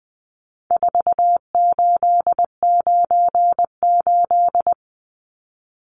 Morse code, heard as a keyed tone, about 20 wpm, 700 Hz. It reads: 4898
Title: Numbers - 4-Digit 20wpm